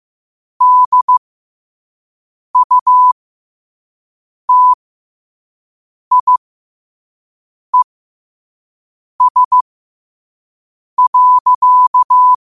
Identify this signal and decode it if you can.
Morse code, heard as a keyed tone, about 15 wpm, spaced as for 5 wpm, 1000 Hz. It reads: DUTIES.
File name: qotdfiles / 5wpm